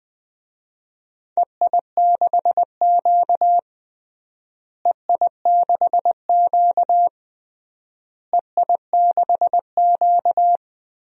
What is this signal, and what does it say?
Morse code, heard as a keyed tone, about 20 wpm, 700 Hz. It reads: EI6Q EI6Q EI6Q
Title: DXCC-100 Call Signs - 4 Characters - Repeated 3x - 20wpm